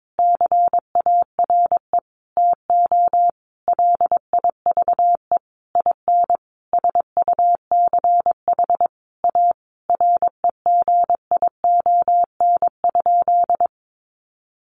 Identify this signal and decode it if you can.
Morse code, heard as a keyed tone, about 22 wpm, 700 Hz. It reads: CARE TO LIVE IN SUCH A REGION?
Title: Sentences from Top 1500 Words 22wpm